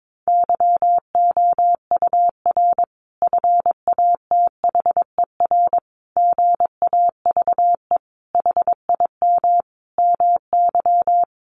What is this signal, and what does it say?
Morse code, heard as a keyed tone, about 22 wpm, 700 Hz. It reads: YOUR FATHER GAVE HIM MY
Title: Sentences from Top 1400 Words 22wpm